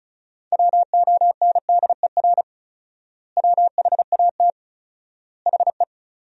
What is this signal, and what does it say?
Morse code, heard as a keyed tone, about 34 wpm, 700 Hz. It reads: WONDER WHAT HE
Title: 4x Word-Spacing - Sentences from Top 500 Words - 35wpm